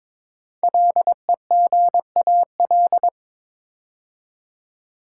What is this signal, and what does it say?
Morse code, heard as a keyed tone, about 22 wpm, 700 Hz. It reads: LEGAL
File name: 5 Letter Words - First 3 Letters Given - 22wpm